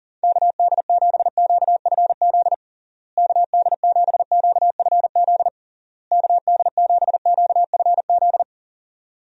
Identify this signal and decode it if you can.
Morse code, heard as a keyed tone, about 40 wpm, 700 Hz. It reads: KD7QFZ KD7QFZ KD7QFZ
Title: USA Call Signs - 6 Characters - Repeated 3x - 40wpm